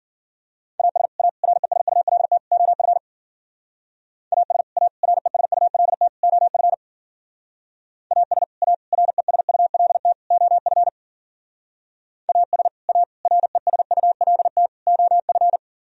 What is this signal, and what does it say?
Morse code, heard as a keyed tone, about 49 wpm, 700 Hz. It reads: AS A RESULT OF AS A RESULT OF AS A RESULT OF AS A RESULT OF
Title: Speed-Racing - Sets of 4 Words 40wpm